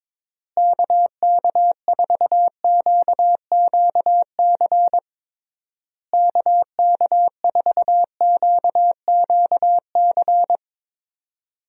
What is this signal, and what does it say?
Morse code, heard as a keyed tone, about 22 wpm, 700 Hz. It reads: KK4QQC KK4QQC
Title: USA Call Signs - 6 Characters - Repeated 3x - 22wpm